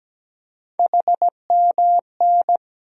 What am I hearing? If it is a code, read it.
Morse code, heard as a keyed tone, about 17 wpm, 700 Hz. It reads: HMN
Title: USA Call Sign Postfixes - 3 Letters 17wpm